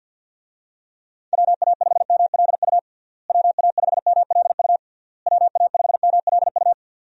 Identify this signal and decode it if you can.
Morse code, heard as a keyed tone, about 52 wpm, 700 Hz. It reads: WA5MLU WA5MLU WA5MLU
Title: USA Call Signs - 6 Characters - Repeated 3x - 50wpm